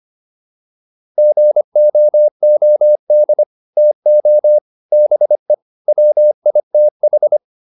Morse code, heard as a keyed tone, about 25 wpm, 600 Hz. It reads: GOOD TO BE WITH